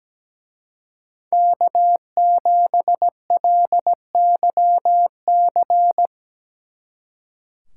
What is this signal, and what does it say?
Morse code, heard as a keyed tone, about 19 wpm, 700 Hz. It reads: K7LYC
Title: Speed Racing - USA 1x3 Call Signs 17wpm